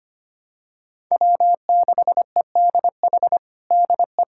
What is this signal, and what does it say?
Morse code, heard as a keyed tone, about 25 wpm, 700 Hz. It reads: W6EDH DE
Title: USA Callsign DE Callsign 25wpm